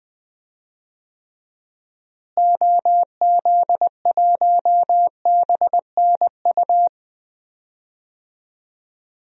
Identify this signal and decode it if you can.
Morse code, heard as a keyed tone, about 20 wpm, 700 Hz. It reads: OZ1BNU
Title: DXCC-100 Call Signs - 6 Characters - Repeated 2x - 20wpm